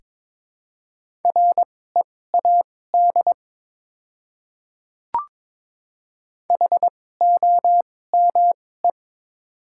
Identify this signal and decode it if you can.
Morse code, heard as a keyed tone, about 22 wpm, spaced as for 16 wpm, 690 Hz. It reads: READ HOME
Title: 2x Character Spacing - Top 200 Words - 22wpm